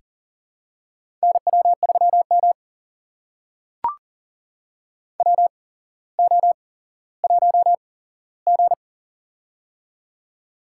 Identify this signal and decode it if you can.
Morse code, heard as a keyed tone, about 39 wpm, spaced as for 17 wpm, 700 Hz. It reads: NW3M WO1G
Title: USA Call Signs - 4 Characters - 8x Character Spacing - 40wpm